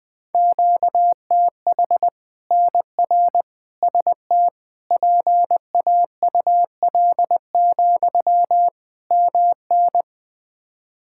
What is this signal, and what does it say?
Morse code, heard as a keyed tone, about 20 wpm, 700 Hz. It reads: QTH NR ST PAUL, MN
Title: K1USN SST QSO Practice 20wpm